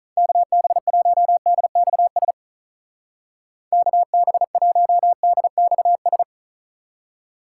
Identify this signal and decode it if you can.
Morse code, heard as a keyed tone, about 43 wpm, 700 Hz. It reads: KB1DXS KB1DXS
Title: Speed Racing - USA 2x3 Call Signs 35wpm